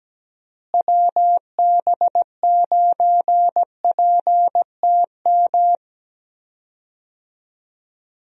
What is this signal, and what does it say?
Morse code, heard as a keyed tone, about 17 wpm, 700 Hz. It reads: WB9PTM
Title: USA Call Signs - 6 Characters - Repeated 3x - 17wpm